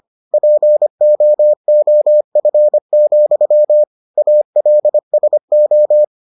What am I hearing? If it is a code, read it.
Morse code, heard as a keyed tone, about 25 wpm, 600 Hz. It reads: POOF, ALSO